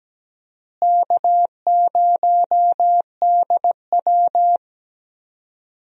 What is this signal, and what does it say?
Morse code, heard as a keyed tone, about 17 wpm, 700 Hz. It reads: K0DW
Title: USA Call Signs - 4 Characters - Repeated 3x - 17wpm